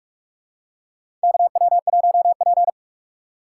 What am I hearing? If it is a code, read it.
Morse code, heard as a keyed tone, about 43 wpm, 700 Hz. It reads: KW1P
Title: USA Call Signs - 4 Characters - 4x Character Spacing - 45wpm